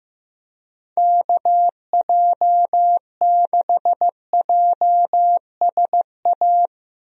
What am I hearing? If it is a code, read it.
Morse code, heard as a keyed tone, about 15 wpm, 700 Hz. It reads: KJ6JSA
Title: USA Call Signs - 6 Characters - Repeated 3x - 15wpm